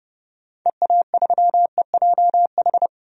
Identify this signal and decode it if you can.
Morse code, heard as a keyed tone, about 30 wpm, 700 Hz. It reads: EA3EJH
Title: DXCC-100 Call Signs - 6 Characters - 30wpm